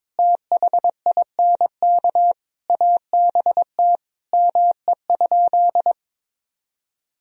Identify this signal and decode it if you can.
Morse code, heard as a keyed tone, about 22 wpm, 700 Hz. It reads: THINK ABT ME?
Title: Sentences from Top 200 Words and Abbreviations 22wpm